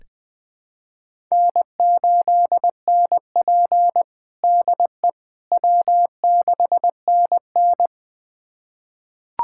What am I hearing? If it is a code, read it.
Morse code, heard as a keyed tone, about 19 wpm, 700 Hz. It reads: N8NP DE W6NN
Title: USA Callsign DE Callsign - 4 Characters - 20wpm